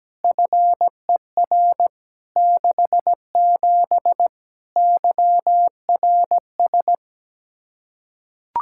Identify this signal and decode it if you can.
Morse code, heard as a keyed tone, about 17 wpm, 700 Hz. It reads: FER 67 YRS
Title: Practice QSO - Exchange 3 - More Info - 17wpm